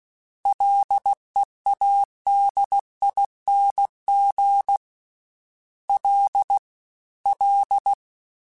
Morse code, heard as a keyed tone, about 16 wpm, 800 Hz. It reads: LEADING L L
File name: F and L / F and L mp3 / F and L Prefixes 3x2 Twice @15